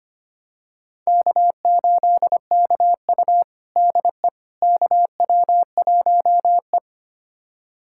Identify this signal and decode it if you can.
Morse code, heard as a keyed tone, about 25 wpm, 700 Hz. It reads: K8KU DE KW1E
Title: USA Callsign DE Callsign - 4 Characters - 25wpm